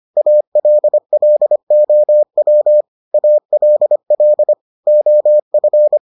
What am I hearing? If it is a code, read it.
Morse code, heard as a keyed tone, about 25 wpm, 600 Hz. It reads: ALLOW ALL OF